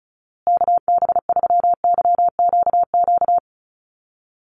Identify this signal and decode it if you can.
Morse code, heard as a keyed tone, about 34 wpm, 700 Hz. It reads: KB3YQQ
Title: USA Call Signs Repeated 3x 35wpm